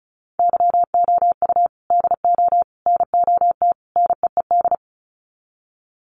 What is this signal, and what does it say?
Morse code, heard as a keyed tone, about 34 wpm, 700 Hz. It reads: YOU DO NOT NEED
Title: Sets of 4 Words 35wpm